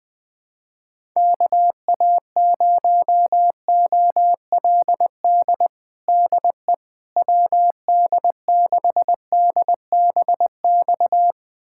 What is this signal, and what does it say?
Morse code, heard as a keyed tone, about 20 wpm, 700 Hz. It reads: KA0OLD DE WD6DBX
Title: USA Callsign DE Callsign - 6 Characters - 20wpm